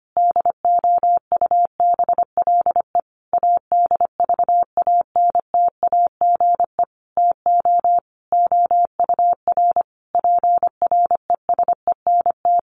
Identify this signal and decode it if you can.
Morse code, heard as a keyed tone, about 25 wpm, 700 Hz. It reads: DOUBLE ADVANTAGE TO OUR PRESENT
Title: Sentences from Top 1400 Words 25wpm